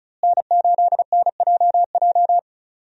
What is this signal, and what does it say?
Morse code, heard as a keyed tone, about 36 wpm, 700 Hz. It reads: N8NJJ
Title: USA Call Signs - 5 Characters - Repeated 2x - 35wpm